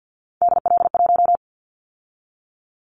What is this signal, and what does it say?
Morse code, heard as a keyed tone, about 50 wpm, 700 Hz. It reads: DL1
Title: Callsign Prefixes for Top 25 DXCC - 50wpm